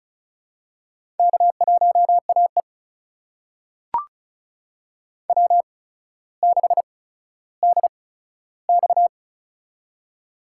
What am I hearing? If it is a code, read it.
Morse code, heard as a keyed tone, about 36 wpm, spaced as for 15 wpm, 700 Hz. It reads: K1AE W6DX
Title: USA Call Signs - 4 Characters - 8x Character Spacing - 35wpm